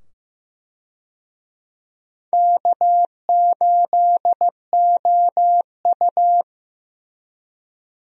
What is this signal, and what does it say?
Morse code, heard as a keyed tone, about 15 wpm, 700 Hz. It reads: K8OU
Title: USA Call Signs - 4 Characters - 15wpm